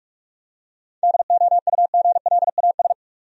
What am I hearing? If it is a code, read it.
Morse code, heard as a keyed tone, about 46 wpm, 700 Hz. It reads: DOUGLAS
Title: USA Common Names - 45wpm